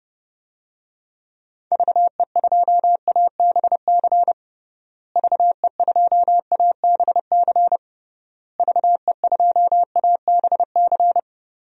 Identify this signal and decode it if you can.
Morse code, heard as a keyed tone, about 29 wpm, 700 Hz. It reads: VE2ABC VE2ABC VE2ABC
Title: DXCC-100 Call Signs - 6 Characters - Repeated 3x - 30wpm